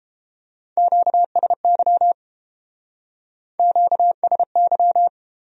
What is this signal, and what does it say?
Morse code, heard as a keyed tone, about 30 wpm, 700 Hz. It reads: QSY QSY